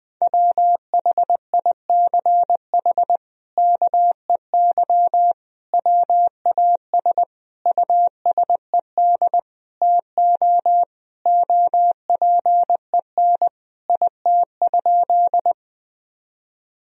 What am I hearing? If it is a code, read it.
Morse code, heard as a keyed tone, about 21 wpm, 700 Hz. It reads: WHICH KEY WAS USED TO OPEN IT?
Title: Sentences from Top 1000 Words - Review Part 4 - 20wpm